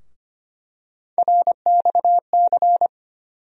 Morse code, heard as a keyed tone, about 25 wpm, 700 Hz. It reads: RXC
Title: USA Call Sign Postfixes - 3 Letters 25wpm